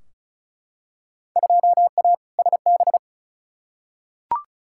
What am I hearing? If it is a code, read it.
Morse code, heard as a keyed tone, about 36 wpm, 700 Hz. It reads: 2A SB
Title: Field Day Exchanges - 35wpm